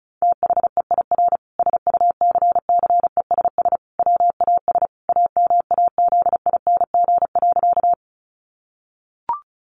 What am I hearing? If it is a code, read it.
Morse code, heard as a keyed tone, about 34 wpm, 700 Hz. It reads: THEIR SUCCESS WAS AMAZING.
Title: Sentences from Top 1500 Words 35wpm